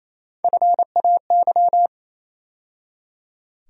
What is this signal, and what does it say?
Morse code, heard as a keyed tone, about 27 wpm, 700 Hz. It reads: FAY
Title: USA Call Sign Postfixes - 3 Letters 28wpm